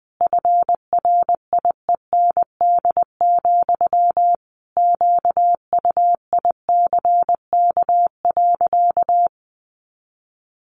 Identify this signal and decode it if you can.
Morse code, heard as a keyed tone, about 20 wpm, 700 Hz. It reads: FRIEND, QUICK.
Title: Sentences from Top 500 Words 20wpm